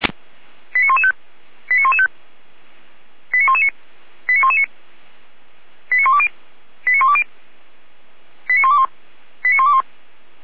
Five-tone selective call (ZVEI), heard as quick runs of five tones, tones of about 80 ms. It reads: 89115 89115 89119 89119 89120 89120 89121 89121